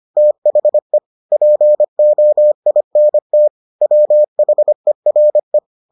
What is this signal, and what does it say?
Morse code, heard as a keyed tone, about 25 wpm, 600 Hz. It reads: THE POINT WHERE